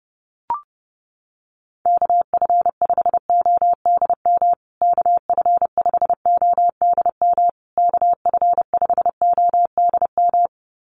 Morse code, heard as a keyed tone, about 30 wpm, 700 Hz. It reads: KF5ODM KF5ODM KF5ODM
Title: USA Call Signs Repeated 3x 30wpm